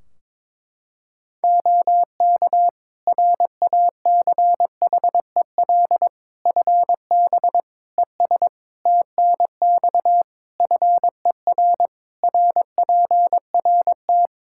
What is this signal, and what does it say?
Morse code, heard as a keyed tone, about 21 wpm, 700 Hz. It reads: OK RACHEL FB ES TNX FER RPRT
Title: Practice QSO - Exchange 2 - Info - 22wpm